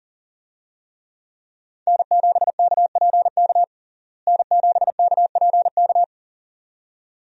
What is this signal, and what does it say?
Morse code, heard as a keyed tone, about 37 wpm, 700 Hz. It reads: N7KPK N7KPK
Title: USA Call Signs - 5 Characters - Repeated 2x - 40wpm